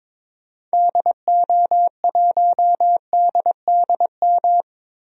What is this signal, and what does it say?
Morse code, heard as a keyed tone, about 22 wpm, 700 Hz. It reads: DO1DDM
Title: DXCC-100 Call Signs - 6 Characters - 22wpm